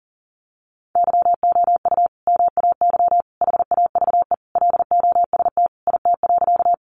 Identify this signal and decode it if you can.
Morse code, heard as a keyed tone, about 37 wpm, 700 Hz. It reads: YOU MAY HAVE LOST IT.
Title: Sentences from Top 1400 Words 40wpm